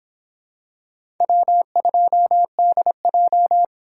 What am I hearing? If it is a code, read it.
Morse code, heard as a keyed tone, about 23 wpm, 700 Hz. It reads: W2DJ